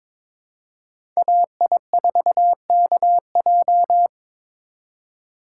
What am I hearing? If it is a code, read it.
Morse code, heard as a keyed tone, about 22 wpm, 700 Hz. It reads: AI4KJ
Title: USA Call Signs - 5 Characters - 2x Character Spacing - 22wpm